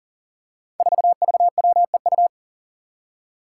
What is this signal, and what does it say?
Morse code, heard as a keyed tone, about 37 wpm, 700 Hz. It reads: 4VWEU
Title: Five Challenging Letter-Number Combinations 40wpm